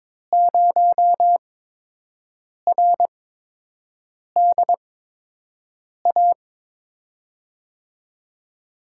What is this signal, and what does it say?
Morse code, heard as a keyed tone, about 23 wpm, spaced as for 6 wpm, 700 Hz. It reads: 0RDA
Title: USA Call Signs - 6 Characters - 8x Character Spacing - 22wpm